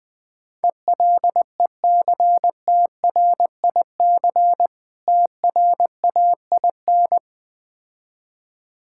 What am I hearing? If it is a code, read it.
Morse code, heard as a keyed tone, about 19 wpm, 700 Hz. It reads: ELECTRIC TRAIN
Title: Difficult Words with Hints - 20wpm